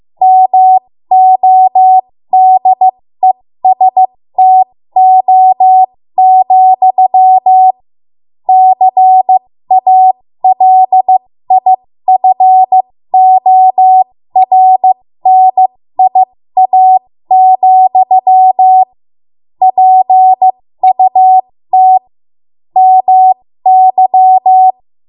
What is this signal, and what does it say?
Morse code, heard as a keyed tone, about 15 wpm, spaced as for 13 wpm, 750 Hz. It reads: MODESTO, CALIFORNIA, PUT MY
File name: Morse / 13WPM